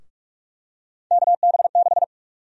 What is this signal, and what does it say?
Morse code, heard as a keyed tone, about 45 wpm, 700 Hz. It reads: KB6
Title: USA Call Sign Prefixes - 45wpm